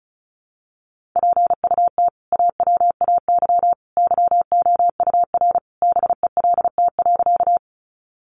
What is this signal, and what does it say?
Morse code, heard as a keyed tone, about 34 wpm, 700 Hz. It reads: PUT AWAY YOUR BELT.
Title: Sentences from Top 1400 Words 35wpm